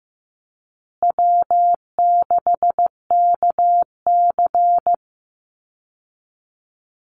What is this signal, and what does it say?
Morse code, heard as a keyed tone, about 15 wpm, 700 Hz. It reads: W6KC
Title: Easy Call Signs Repeated 3x - 15wpm